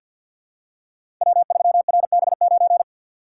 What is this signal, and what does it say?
Morse code, heard as a keyed tone, about 48 wpm, 700 Hz. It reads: W3RB9
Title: Code Groups - 5 Letter:Numbers - 50wpm